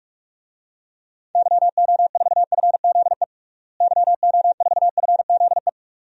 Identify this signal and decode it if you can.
Morse code, heard as a keyed tone, about 46 wpm, 700 Hz. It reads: YO4FZE YO4FZE
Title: DXCC-100 Call Signs - 6 Characters - Repeated 2x - 45wpm